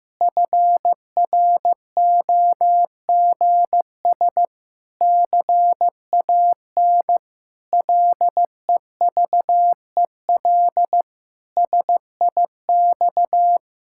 Morse code, heard as a keyed tone, about 15 wpm, 700 Hz. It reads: FROGS CAN LEVEL SIX
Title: Pangram Challenge v2 - 15wpm